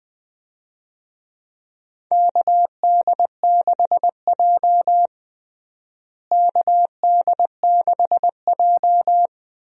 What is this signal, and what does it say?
Morse code, heard as a keyed tone, about 20 wpm, 700 Hz. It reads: KD6J KD6J
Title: USA Call Signs - 4 Characters - Repeated 3x - 20wpm